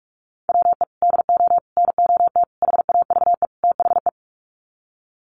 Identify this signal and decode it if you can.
Morse code, heard as a keyed tone, about 44 wpm, 700 Hz. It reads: WE DO NOT HAVE THE
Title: Sets of 5 Words 45wpm